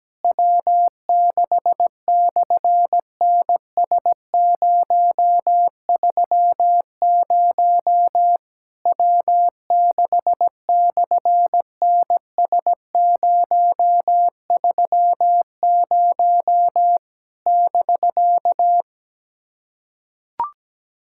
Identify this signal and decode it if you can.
Morse code, heard as a keyed tone, about 17 wpm, 700 Hz. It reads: W6/NS030 W6/NS030 <BK>
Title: SOTA QSO Practice 17wpm